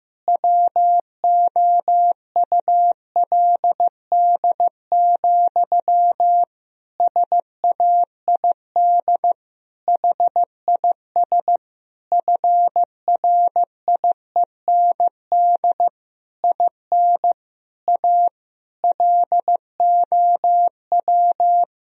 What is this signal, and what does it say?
Morse code, heard as a keyed tone, about 15 wpm, 700 Hz. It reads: WOULD, SAID HIS FRIEND IN A LOW
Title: Sentences from Top 1000 Words - Review Part 1 - 15wpm